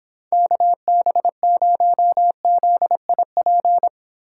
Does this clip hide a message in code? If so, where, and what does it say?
Yes: Morse code from 0.3 to 3.9 s, heard as a keyed tone, about 26 wpm, 700 Hz. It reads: KB0ZIP